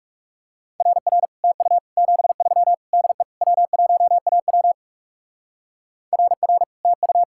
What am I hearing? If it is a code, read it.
Morse code, heard as a keyed tone, about 36 wpm, 700 Hz. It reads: RR TU 73 DE W1AW RR TU
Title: Speed-Racing Common QSO Phrases - 30wpm